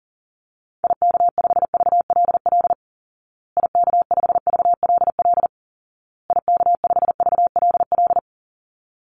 Morse code, heard as a keyed tone, about 40 wpm, 700 Hz. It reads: IK5VLL IK5VLL IK5VLL
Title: Call Signs Repeated 2x - 40wpm